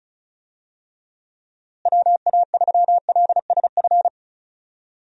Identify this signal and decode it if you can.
Morse code, heard as a keyed tone, about 34 wpm, 700 Hz. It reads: WA3LSF
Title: USA Call Signs - 6 Characters - Repeated 2x - 35wpm